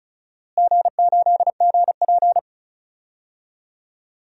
Morse code, heard as a keyed tone, about 36 wpm, 700 Hz. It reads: G8GP
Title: DXCC-100 Call Signs - 4 Characters - Repeated 2x - 35wpm